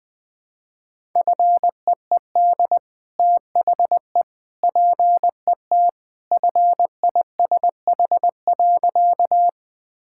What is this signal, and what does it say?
Morse code, heard as a keyed tone, about 20 wpm, 700 Hz. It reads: FEED THE PET FISH.
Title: Easy Sentences with Hints - 20wpm